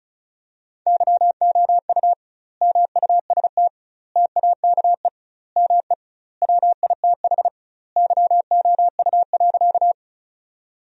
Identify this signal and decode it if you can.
Morse code, heard as a keyed tone, about 36 wpm, 700 Hz. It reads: YOU MUST TAKE ME WITH YOU.
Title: 2x Word-Spacing - Sentences from Top 200 Words - 35wpm